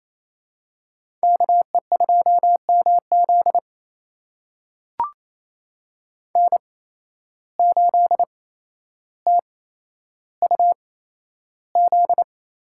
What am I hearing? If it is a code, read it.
Morse code, heard as a keyed tone, about 28 wpm, spaced as for 12 wpm, 700 Hz. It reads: KE2MZ N8TUZ
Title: USA Call Signs - 5 Characters - 8x Character Spacing - 28wpm